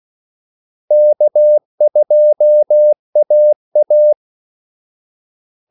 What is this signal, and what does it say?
Morse code, heard as a keyed tone, about 16 wpm, 600 Hz. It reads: K2AA
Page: Kaldesignaler 16 wpm